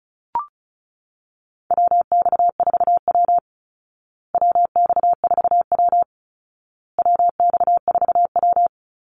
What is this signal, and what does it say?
Morse code, heard as a keyed tone, about 34 wpm, 700 Hz. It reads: WX4W WX4W WX4W
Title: Call Signs Repeated 2x - 35wpm